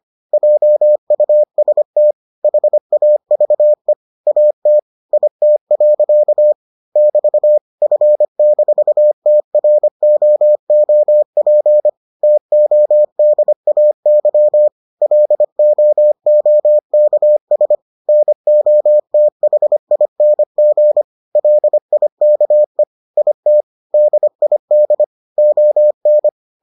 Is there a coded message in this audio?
Yes: Morse code from 0.3 to 26.3 s, heard as a keyed tone, about 25 wpm, 600 Hz. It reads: JUST HAVE AT IT. <BT> F-TROOP TODAY LOOKS NOTHING LIKE IT DID ON